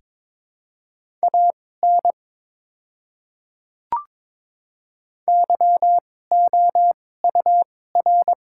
Morse code, heard as a keyed tone, about 22 wpm, spaced as for 16 wpm, 700 Hz. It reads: AN YOUR
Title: 2x Character Spacing - Top 300 Words - 22wpm